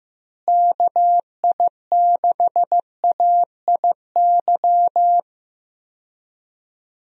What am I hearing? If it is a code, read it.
Morse code, heard as a keyed tone, about 15 wpm, 700 Hz. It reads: KI6AIY
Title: USA Call Signs - 6 Characters - Repeated 3x - 15wpm